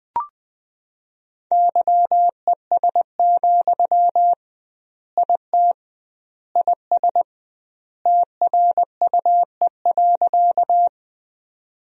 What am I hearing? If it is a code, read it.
Morse code, heard as a keyed tone, about 20 wpm, 700 Hz. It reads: YES, IT IS TRUE.
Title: 2x Word-Spacing - Sentences from Top 500 Words - 20wpm